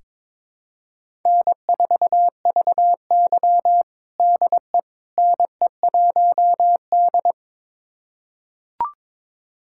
Morse code, heard as a keyed tone, about 22 wpm, 700 Hz. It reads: N4VY DE NE1D
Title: USA Callsign DE Callsign - 4 Characters - 22wpm